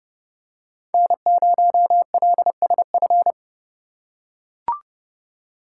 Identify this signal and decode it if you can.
Morse code, heard as a keyed tone, about 28 wpm, 700 Hz. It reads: N0LSF